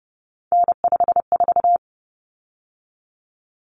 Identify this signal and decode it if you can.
Morse code, heard as a keyed tone, about 30 wpm, 700 Hz. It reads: N54
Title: Common Cut Numbers - 3-Digit 30wpm